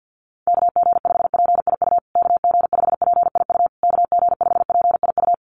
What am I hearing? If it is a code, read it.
Morse code, heard as a keyed tone, about 50 wpm, 700 Hz. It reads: KG5PIU KG5PIU KG5PIU
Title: USA Call Signs Repeated 3x 50wpm